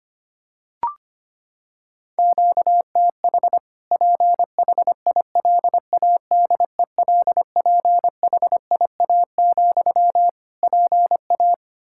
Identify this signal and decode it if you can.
Morse code, heard as a keyed tone, about 25 wpm, 700 Hz. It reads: QTH PHILADELPHIA, PA
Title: Practice QSO - Exchange 1 - Beginning - 25wpm